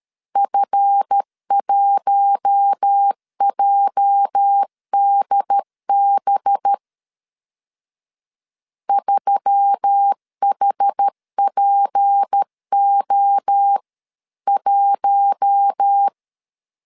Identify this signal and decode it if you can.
Morse code, heard as a keyed tone, about 13 wpm, 800 Hz. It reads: F1JDB 3HPO1
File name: LEZIONI / 8 YJ45 / 8 YJ 12wpm 2235